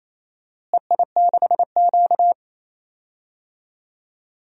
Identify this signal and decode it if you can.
Morse code, heard as a keyed tone, about 28 wpm, 700 Hz. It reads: EI6Q
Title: DXCC-100 Call Signs - 4 Characters - Repeated 3x - 28wpm